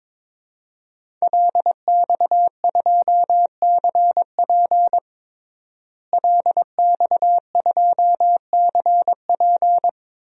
Morse code, heard as a keyed tone, about 22 wpm, 700 Hz. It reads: LX2CP LX2CP
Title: DXCC-100 Call Signs - 5 Characters - Repeated 2x - 22wpm